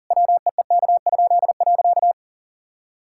Morse code, heard as a keyed tone, about 40 wpm, 700 Hz. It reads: WEEK?.
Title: Pangram Challenge v2 - 40wpm